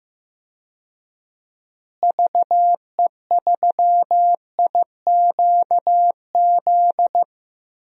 Morse code, heard as a keyed tone, about 15 wpm, 700 Hz. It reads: VE3IQZ
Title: DXCC-100 Call Signs - 6 Characters - Repeated 3x - 15wpm